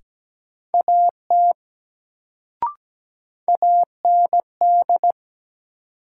Morse code, heard as a keyed tone, about 17 wpm, 700 Hz. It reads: AT AND
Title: Sentences from Top 100 Words - New Word Drill - 17wpm